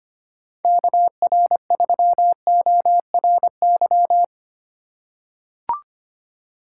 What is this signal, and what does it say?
Morse code, heard as a keyed tone, about 25 wpm, 700 Hz. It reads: KR3ORY
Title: USA Call Signs - 6 Characters - 8x Character Spacing - 25wpm